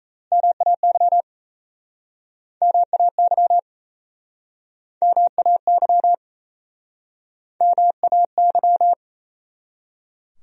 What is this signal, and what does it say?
Morse code, heard as a keyed tone, about 37 wpm, 700 Hz. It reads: MAY MAY MAY MAY